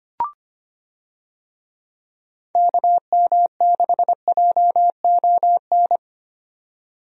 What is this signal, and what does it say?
Morse code, heard as a keyed tone, about 25 wpm, 700 Hz. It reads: KM6JON
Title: USA Call Signs - 6 Characters - 25wpm